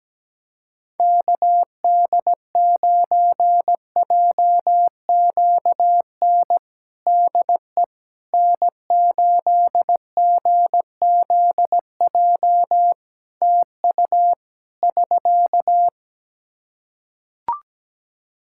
Morse code, heard as a keyed tone, about 17 wpm, 700 Hz. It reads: KD9JQN DE N8GZJ TU <SK>
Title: K1USN SST QSO Practice 17wpm